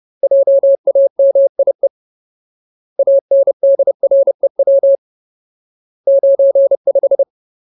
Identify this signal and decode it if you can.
Morse code, heard as a keyed tone, about 30 wpm, 550 Hz. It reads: JAMIE ANDREW 95
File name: Ultimate Word List / 30WPM Ultimate